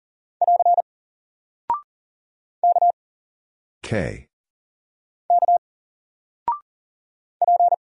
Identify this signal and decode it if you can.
Morse code, heard as a keyed tone, about 41 wpm, 700 Hz. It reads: <AR> K K P
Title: Single Character (Letter:Number:Punctuation:Prosign) 40wpm